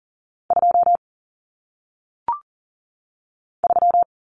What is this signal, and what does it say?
Morse code, heard as a keyed tone, about 40 wpm, 700 Hz. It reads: 2 3
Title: Common Cut Numbers - 1-Digit 40wpm